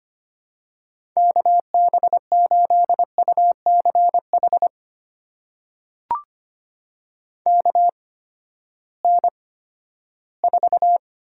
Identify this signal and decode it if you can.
Morse code, heard as a keyed tone, about 25 wpm, spaced as for 14 wpm, 700 Hz. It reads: KB8UCH KN4
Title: USA Call Signs - 6 Characters - 8x Character Spacing - 25wpm